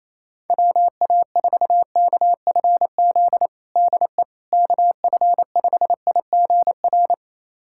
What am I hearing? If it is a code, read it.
Morse code, heard as a keyed tone, about 28 wpm, 700 Hz. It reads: WA4KFZ DE KF5IGR
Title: USA Callsign DE Callsign - 6 Characters - 28wpm